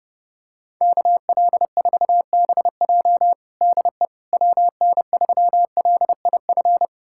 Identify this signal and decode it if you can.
Morse code, heard as a keyed tone, about 30 wpm, 700 Hz. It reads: KL4BJ DE WN3LIF
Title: USA Callsign DE Callsign 30wpm